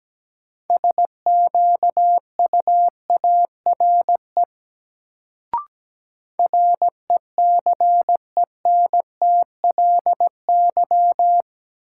Morse code, heard as a keyed tone, about 17 wpm, 700 Hz. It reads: SQUARE RECENTLY
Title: Sentences from Top 1200 Words - New Word Drill Alt - 17wpm